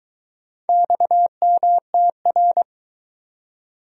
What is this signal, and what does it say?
Morse code, heard as a keyed tone, about 22 wpm, 700 Hz. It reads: XMTR